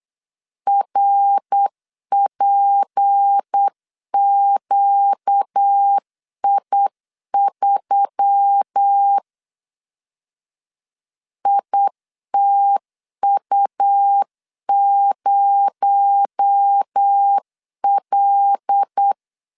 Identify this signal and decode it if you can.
Morse code, heard as a keyed tone, about 8 wpm, 800 Hz. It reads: RPQI3 ITU0L
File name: CW / DET / 1 6 1500C S C 8wpm 5219